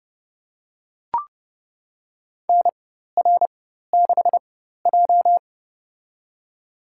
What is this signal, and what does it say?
Morse code, heard as a keyed tone, about 32 wpm, spaced as for 14 wpm, 700 Hz. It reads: NR6J
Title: USA Call Signs - 4 Characters - 4x Character Spacing - 30wpm